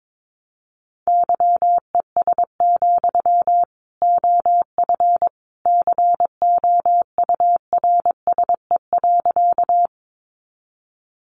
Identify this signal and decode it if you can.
Morse code, heard as a keyed tone, about 22 wpm, 700 Hz. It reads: YES, OF COURSE.
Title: Sentences from Top 500 Words 22wpm